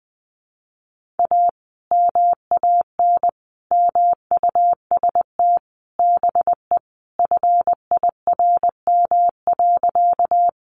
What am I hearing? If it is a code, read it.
Morse code, heard as a keyed tone, about 20 wpm, 700 Hz. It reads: A MAN MUST BE FIRM.
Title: Sentences from Top 1400 Words 20wpm